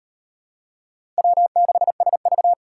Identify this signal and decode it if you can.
Morse code, heard as a keyed tone, about 38 wpm, 700 Hz. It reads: W6SV